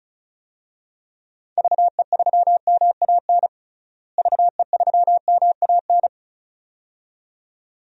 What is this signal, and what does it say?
Morse code, heard as a keyed tone, about 36 wpm, 700 Hz. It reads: VE3MAN VE3MAN
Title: DXCC-100 Call Signs - 6 Characters - Repeated 2x - 35wpm